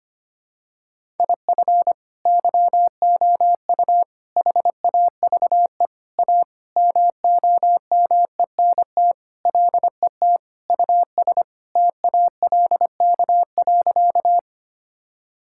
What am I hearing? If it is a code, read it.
Morse code, heard as a keyed tone, about 25 wpm, 700 Hz. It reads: IF YOU HAVE A MOMENT LET US TALK.
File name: practice / 25wpm / Sentences with 4-Word Phrases - 25wpm